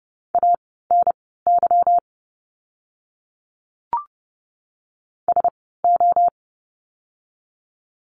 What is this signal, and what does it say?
Morse code, heard as a keyed tone, about 30 wpm, spaced as for 17 wpm, 700 Hz. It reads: ANY SO
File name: course / 30:10wpm / 230 Lesson 37f - Letter x - Words 30wpm